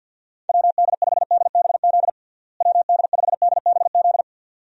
Morse code, heard as a keyed tone, about 50 wpm, 700 Hz. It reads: WD5DBZ WD5DBZ
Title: USA Call Signs - 6 Characters - Repeated 2x - 50wpm